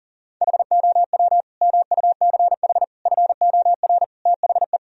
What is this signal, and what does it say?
Morse code, heard as a keyed tone, about 40 wpm, 700 Hz. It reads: HOW MUCH FOR THE
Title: Sentences from Top 1000 Words - Review Part 4 - 40wpm